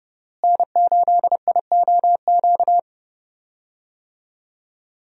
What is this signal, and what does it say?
Morse code, heard as a keyed tone, about 30 wpm, 700 Hz. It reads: N8IOQ
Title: USA Call Signs - 5 Characters - Repeated 2x - 30wpm